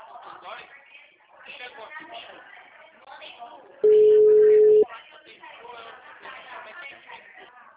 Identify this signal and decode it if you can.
Ringback tone (Europe, 425 Hz), with speech.
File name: out- -2009- - - .49067